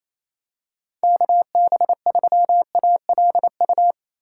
Morse code, heard as a keyed tone, about 28 wpm, 700 Hz. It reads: KB3ALU
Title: USA Call Signs - 6 Characters - 3x Character Spacing - 28wpm